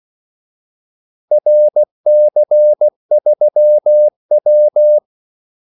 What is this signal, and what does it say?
Morse code, heard as a keyed tone, about 16 wpm, 600 Hz. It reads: RC3W
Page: Kaldesignaler 16 wpm